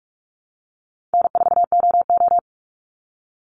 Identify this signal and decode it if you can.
Morse code, heard as a keyed tone, about 44 wpm, 700 Hz. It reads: N4OO
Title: Call Signs Repeated 2x - 45wpm